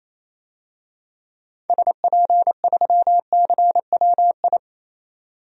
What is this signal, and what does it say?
Morse code, heard as a keyed tone, about 28 wpm, 700 Hz. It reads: SP3CWI
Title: DXCC-100 Call Signs - 6 Characters - 28wpm